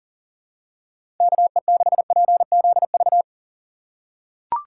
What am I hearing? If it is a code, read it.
Morse code, heard as a keyed tone, about 40 wpm, 700 Hz. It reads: KE6PZV
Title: USA Call Signs - 6 Characters - 4x Character Spacing - 40wpm